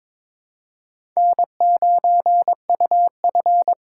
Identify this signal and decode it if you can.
Morse code, heard as a keyed tone, about 22 wpm, 700 Hz. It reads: N9UF
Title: USA Call Signs - 4 Characters - 4x Character Spacing - 22wpm